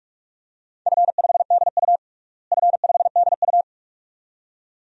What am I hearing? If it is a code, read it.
Morse code, heard as a keyed tone, about 46 wpm, 700 Hz. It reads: F5DU F5DU
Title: DXCC-100 Call Signs - 4 Characters - Repeated 2x - 45wpm